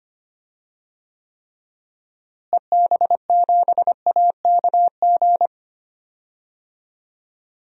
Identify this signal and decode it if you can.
Morse code, heard as a keyed tone, about 25 wpm, 700 Hz. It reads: EB7AKG
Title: DXCC-100 Call Signs - 6 Characters - Repeated 3x - 25wpm